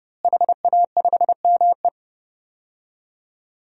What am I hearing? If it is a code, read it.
Morse code, heard as a keyed tone, about 30 wpm, 700 Hz. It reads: HA5ME
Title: DXCC-100 Call Signs - 5 Characters - Repeated 3x - 30wpm